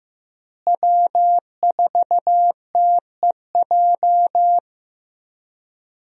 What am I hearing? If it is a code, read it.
Morse code, heard as a keyed tone, about 15 wpm, 700 Hz. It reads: W4TEJ
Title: USA Call Signs - 5 Characters - Repeated 3x - 15wpm